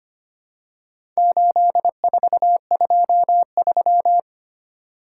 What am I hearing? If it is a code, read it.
Morse code, heard as a keyed tone, about 25 wpm, 700 Hz. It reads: 8423
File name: practice / 25wpm / Four Letter-Number Combinations - 25wpm